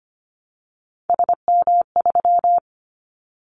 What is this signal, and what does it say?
Morse code, heard as a keyed tone, about 25 wpm, 700 Hz. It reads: SM3
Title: Callsign Prefixes for Top 25 DXCC - 25wpm